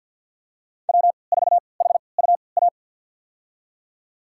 Morse code, heard as a keyed tone, about 48 wpm, spaced as for 28 wpm, 700 Hz. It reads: W4HUA
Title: USA Call Signs - 3x Character Spacing - 50wpm